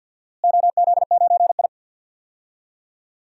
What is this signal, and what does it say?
Morse code, heard as a keyed tone, about 51 wpm, 700 Hz. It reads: OZ9I
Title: DXCC-100 Call Signs - 4 Characters - Repeated 3x - 50wpm